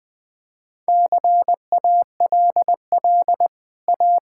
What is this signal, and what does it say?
Morse code, heard as a keyed tone, about 20 wpm, 700 Hz. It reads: CALL A
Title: Easy Sentences with Hints - 20wpm